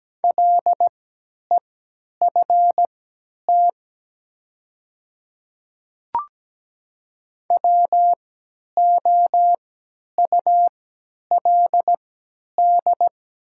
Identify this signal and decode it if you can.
Morse code, heard as a keyed tone, about 17 wpm, spaced as for 10 wpm, 690 Hz. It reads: LEFT WOULD
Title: 3x Character Spacing - Top 300 Words - 17wpm